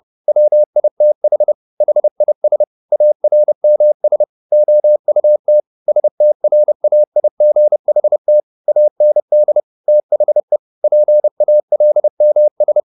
Morse code, heard as a keyed tone, about 30 wpm, 600 Hz. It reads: WITH HIS ARMS OUT STRAIGHT AND THE PALMS